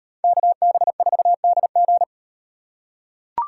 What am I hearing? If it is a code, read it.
Morse code, heard as a keyed tone, about 38 wpm, 700 Hz. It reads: KB4DG